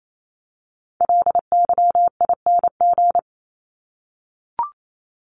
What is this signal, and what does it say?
Morse code, heard as a keyed tone, about 28 wpm, 700 Hz. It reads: LYING
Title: ING Suffix - Top 100 Words with Base Words 28wpm